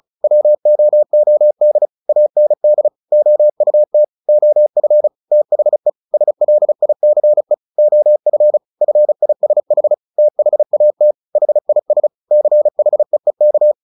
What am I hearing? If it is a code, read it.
Morse code, heard as a keyed tone, about 34 wpm, 600 Hz. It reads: WOOD AND OUT OF THE SLICE OF FISH THAT HIS CHEEK
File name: the old man and the sea 2 35wpm